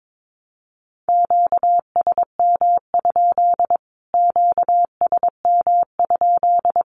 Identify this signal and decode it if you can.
Morse code, heard as a keyed tone, about 22 wpm, 700 Hz. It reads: QSM? QSM?
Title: Q-Codes Repeated 3x 22wpm